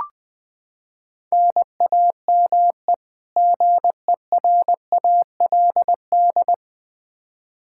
Morse code, heard as a keyed tone, about 20 wpm, 700 Hz. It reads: NAME GERALD
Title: Practice QSO - Exchange 1 - Beginning - 20wpm